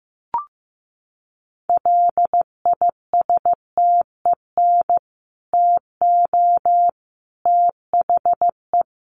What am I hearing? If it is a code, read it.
Morse code, heard as a keyed tone, about 15 wpm, 700 Hz. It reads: LISTEN TO THE
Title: Sentences from Top 1300 Words 15wpm